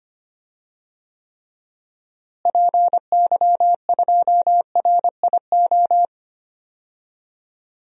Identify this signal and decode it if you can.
Morse code, heard as a keyed tone, about 25 wpm, 700 Hz. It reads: PY2RIO
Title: DXCC-100 Call Signs - 6 Characters - Repeated 3x - 25wpm